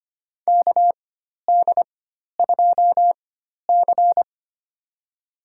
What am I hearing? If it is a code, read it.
Morse code, heard as a keyed tone, about 25 wpm, spaced as for 12 wpm, 700 Hz. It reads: KD2C
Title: USA Call Signs - 4x Character Spacing - 25wpm